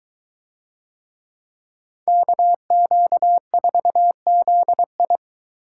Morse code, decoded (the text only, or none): KQ4ZI